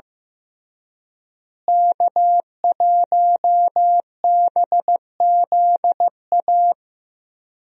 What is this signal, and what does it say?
Morse code, heard as a keyed tone, about 15 wpm, 700 Hz. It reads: K1BZA
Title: USA Call Signs - 5 Characters - Repeated 3x - 15wpm